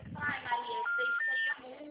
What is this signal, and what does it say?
Special information tone, with speech.